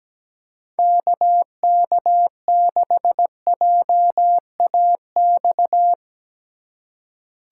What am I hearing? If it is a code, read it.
Morse code, heard as a keyed tone, about 17 wpm, 700 Hz. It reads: KK6JAX
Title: USA Call Signs - 2x Character Spacing - 17wpm